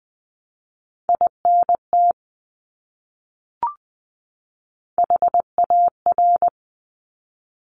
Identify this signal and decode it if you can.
Morse code, heard as a keyed tone, about 20 wpm, 700 Hz. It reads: INT HAR
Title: Top Three Letter Combinations in Words 20wpm